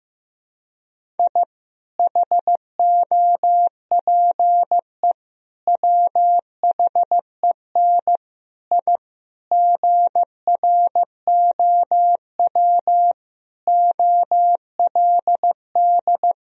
Morse code, heard as a keyed tone, about 15 wpm, 700 Hz. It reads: I HOPE WHEN I GROW OLD
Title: Sentences from Top 1000 Words - Review Part 1 - 15wpm